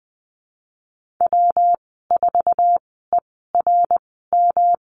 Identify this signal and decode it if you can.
Morse code, heard as a keyed tone, about 20 wpm, spaced as for 15 wpm, 700 Hz. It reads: W4ERM
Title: 161 Lesson 26g - Letter g - Call Signs 20wpm